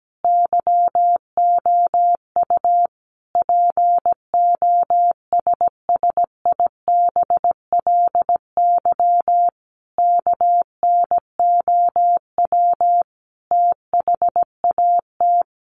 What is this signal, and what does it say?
Morse code, heard as a keyed tone, about 17 wpm, 700 Hz. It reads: YOU POSSIBLY KNOW THAT
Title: Sentences from Top 1300 Words 17wpm